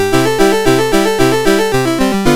There is no code.